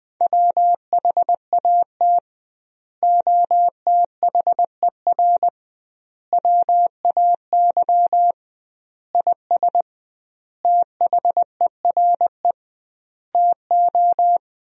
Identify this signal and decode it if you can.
Morse code, heard as a keyed tone, about 21 wpm, 700 Hz. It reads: WHAT OTHER WAY IS THERE TO
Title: 2x Word-Spacing - Sentences from Top 100 Words - 20wpm